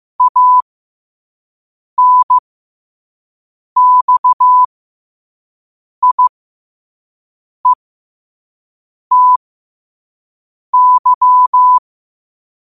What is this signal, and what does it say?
Morse code, heard as a keyed tone, about 15 wpm, spaced as for 5 wpm, 1000 Hz. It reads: ANXIETY
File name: qotdfiles / 5wpm